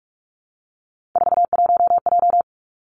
Morse code, heard as a keyed tone, about 46 wpm, 700 Hz. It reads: 41J
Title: Three Challenging Letter-Number Combinations 45wpm